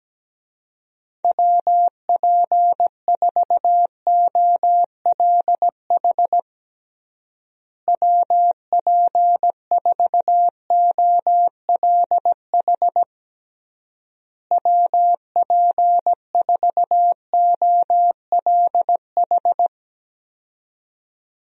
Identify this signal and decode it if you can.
Morse code, heard as a keyed tone, about 17 wpm, 700 Hz. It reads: WP4OLH WP4OLH WP4OLH
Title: USA Call Signs - 6 Characters - Repeated 3x - 17wpm